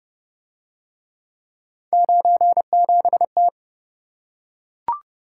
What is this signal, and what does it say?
Morse code, heard as a keyed tone, about 27 wpm, 700 Hz. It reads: 97T